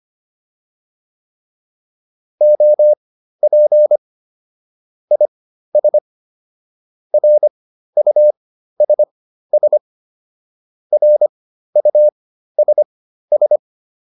Morse code, heard as a keyed tone, about 25 wpm, spaced as for 13 wpm, 600 Hz. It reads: OP IS RUSS RUSS
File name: QSO 207 13